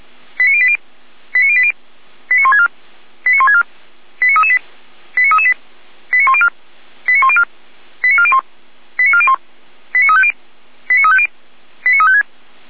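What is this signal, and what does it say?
Five-tone selective call (ZVEI), heard as quick runs of five tones, tones of about 70 ms. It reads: 89080 89080 89164 89164 89338 89338 89104 89104 89401 89401 89350 89350 89356
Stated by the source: ZVEI five-tone call received by SDR, recorded by feuerwehr-orlamuende.org (fire brigade Orlamünde).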